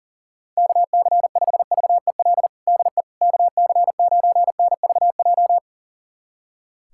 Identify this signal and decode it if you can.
Morse code, heard as a keyed tone, about 40 wpm, 700 Hz. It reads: KC5VEL DE KC9NVJ
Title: USA Callsign DE Callsign - 6 Characters - 40wpm